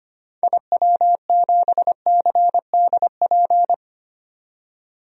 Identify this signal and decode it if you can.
Morse code, heard as a keyed tone, about 25 wpm, 700 Hz. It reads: IW7CDP
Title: DXCC-100 Call Signs - 6 Characters - 25wpm